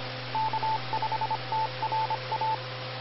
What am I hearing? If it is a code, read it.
Morse code, heard as a keyed tone, about 25 wpm, 880 Hz, with radio noise. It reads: K5TRA
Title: Austin, TX Amateur Repeaters